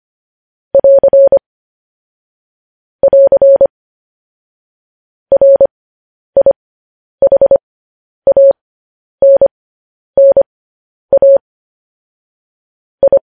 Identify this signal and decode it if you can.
Morse code, heard as a keyed tone, about 25 wpm, spaced as for 10 wpm, 550 Hz. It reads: <AR> <AR> RIHANNA I